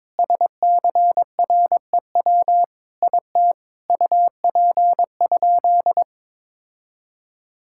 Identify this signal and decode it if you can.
Morse code, heard as a keyed tone, about 22 wpm, 700 Hz. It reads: SCREW IT UP?
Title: Sentences from Top 1600 Words 22wpm